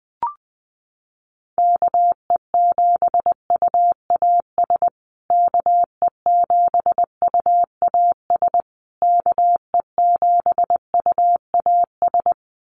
Morse code, heard as a keyed tone, about 20 wpm, 700 Hz. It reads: KE7UAS KE7UAS KE7UAS
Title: USA Call Signs Repeated 3x 20wpm